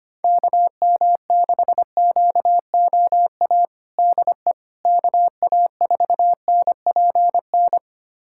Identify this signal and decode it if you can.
Morse code, heard as a keyed tone, about 25 wpm, 700 Hz. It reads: KM6QOA DE KA4NPN
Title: USA Callsign DE Callsign - 6 Characters - 25wpm